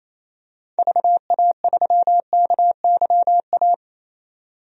Morse code, heard as a keyed tone, about 28 wpm, 700 Hz. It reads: VA3KYA
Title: DXCC-100 Call Signs - 6 Characters - 28wpm